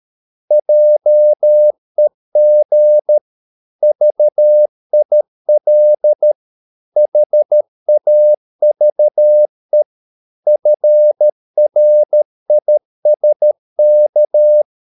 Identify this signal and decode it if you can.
Morse code, heard as a keyed tone, about 13 wpm, 600 Hz. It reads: JEG VIL HAVE FRISK
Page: Ferie 13 wpm